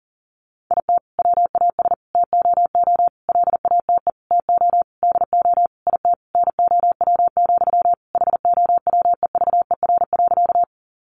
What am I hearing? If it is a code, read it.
Morse code, heard as a keyed tone, about 40 wpm, 700 Hz. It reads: IT WAS TOO LATE TO DO IT NOW, HOWEVER.
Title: Sentences from Top 600 Words 40wpm